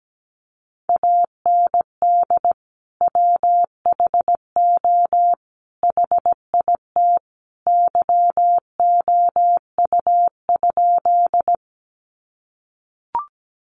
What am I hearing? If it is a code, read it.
Morse code, heard as a keyed tone, about 17 wpm, 700 Hz. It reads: AND WHO HIT YOU?
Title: Sentences from Top 1200 Words 17wpm